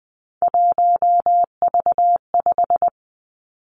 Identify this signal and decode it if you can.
Morse code, heard as a keyed tone, about 20 wpm, 700 Hz. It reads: 1V5
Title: Three Challenging Letter-Number Combinations 20wpm